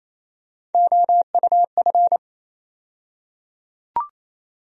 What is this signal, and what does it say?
Morse code, heard as a keyed tone, about 28 wpm, 700 Hz. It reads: OUF
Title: USA Call Sign Postfixes - 3 Letters 28wpm